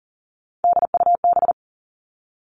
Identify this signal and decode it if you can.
Morse code, heard as a keyed tone, about 42 wpm, 700 Hz. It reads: DUB
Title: Three Challenging Letter-Number Combinations 40wpm